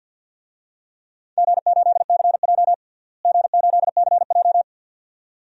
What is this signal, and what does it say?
Morse code, heard as a keyed tone, about 47 wpm, 700 Hz. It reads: G8CJ G8CJ
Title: DXCC-100 Call Signs - 4 Characters - Repeated 2x - 50wpm